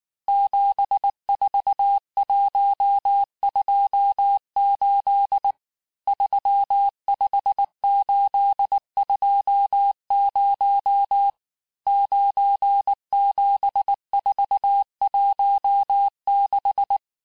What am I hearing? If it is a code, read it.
Morse code, heard as a keyed tone, about 19 wpm, 800 Hz. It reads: 74128 35820 97416
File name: cours-du-lundi files / cours-1583--texte-chiffre-18-mpm